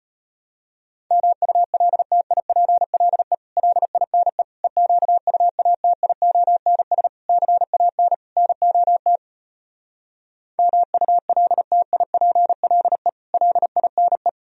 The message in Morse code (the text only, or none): MULTIPLE LINE EQUATIONS CAN NOT MULTIPLE LINE